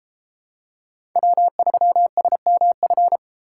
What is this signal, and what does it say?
Morse code, heard as a keyed tone, about 32 wpm, 700 Hz. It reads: W3SMF